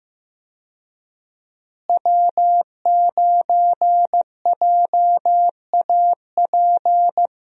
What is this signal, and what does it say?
Morse code, heard as a keyed tone, about 15 wpm, 700 Hz. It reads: W9JAP
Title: USA Call Signs - 5 Characters - Repeated 3x - 15wpm